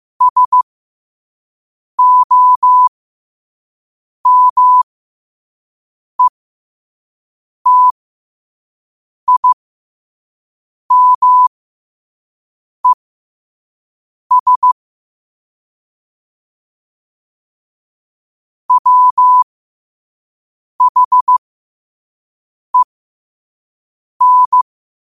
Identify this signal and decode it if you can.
Morse code, heard as a keyed tone, about 15 wpm, spaced as for 5 wpm, 1000 Hz. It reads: SOMETIMES WHEN
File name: qotdfiles / 5wpm